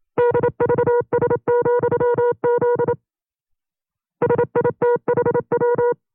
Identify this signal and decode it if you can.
Morse code, heard as a keyed tone, about 27 wpm, 480 Hz. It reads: DVS,Z SITHW
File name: CW-practice/25WPM